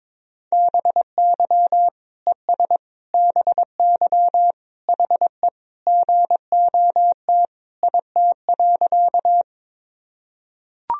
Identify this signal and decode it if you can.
Morse code, heard as a keyed tone, about 22 wpm, 700 Hz. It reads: BY ES BY HE GOT IT.
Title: Sentences from Top 300 Words and Abbreviations 22wpm